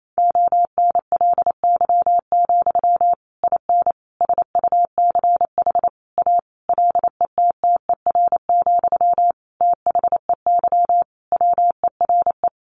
Morse code, heard as a keyed tone, about 28 wpm, 700 Hz. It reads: ONLY, IN SUCH A LETTER, THEY WERE
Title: Sentences from Top 1500 Words 28wpm